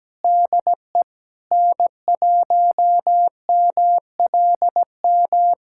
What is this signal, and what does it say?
Morse code, heard as a keyed tone, about 17 wpm, 700 Hz. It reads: DE N1MLM
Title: SOTA QSO Practice 17wpm